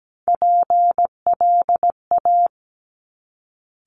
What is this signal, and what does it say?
Morse code, heard as a keyed tone, about 17 wpm, 700 Hz. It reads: PLA
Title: Top Three Letter Combinations in Words 17wpm